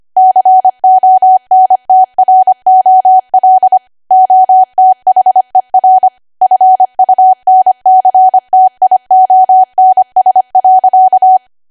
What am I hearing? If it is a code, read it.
Morse code, heard as a keyed tone, about 25 wpm, 750 Hz. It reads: CONTROL OTHER FUNCTIONS.
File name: Archive / 25 WPM / 25WPM